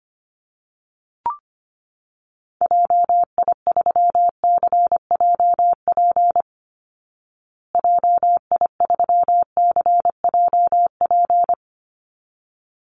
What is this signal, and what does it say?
Morse code, heard as a keyed tone, about 25 wpm, 700 Hz. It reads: JI3CJP JI3CJP
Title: Call Signs Repeated 3x - 25wpm